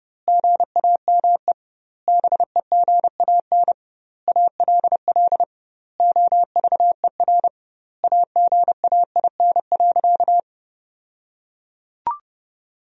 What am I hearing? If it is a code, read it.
Morse code, heard as a keyed tone, about 30 wpm, 700 Hz. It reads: GAME BEGAN ALL OVER AGAIN.
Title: 2x Word-Spacing - Sentences from Top 500 Words - 30wpm